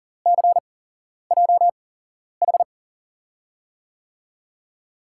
Morse code, heard as a keyed tone, about 40 wpm, spaced as for 11 wpm, 700 Hz. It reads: CJH
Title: USA Call Signs - 6 Characters - 8x Character Spacing - 40wpm